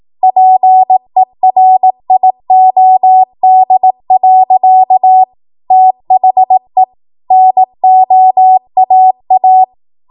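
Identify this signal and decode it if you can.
Morse code, heard as a keyed tone, about 18 wpm, 750 Hz. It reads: PERIOD. THE NOAA
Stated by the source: Code practice at 18 WPM, a transition run bridging 15 to 20 WPM.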